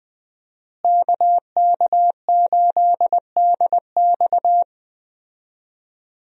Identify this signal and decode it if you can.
Morse code, heard as a keyed tone, about 20 wpm, 700 Hz. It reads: KK8DX
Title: USA Call Signs - 5 Characters - Repeated 2x - 20wpm